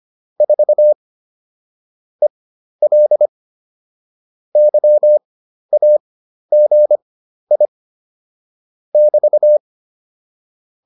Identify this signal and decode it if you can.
Morse code, heard as a keyed tone, about 25 wpm, spaced as for 12 wpm, 600 Hz. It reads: 4 EL YAGI <BT>